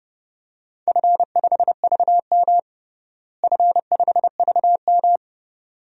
Morse code, heard as a keyed tone, about 32 wpm, 700 Hz. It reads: F5VM F5VM
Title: DXCC-100 Call Signs - 4 Characters - Repeated 2x - 30wpm